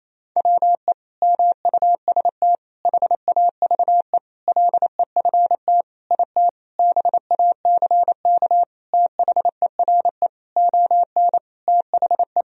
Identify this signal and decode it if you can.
Morse code, heard as a keyed tone, about 28 wpm, 700 Hz. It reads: WE MUST HAVE LEFT IT BACK THERE ON THE
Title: Sentences from Top 1000 Words - Review Part 2 - 28wpm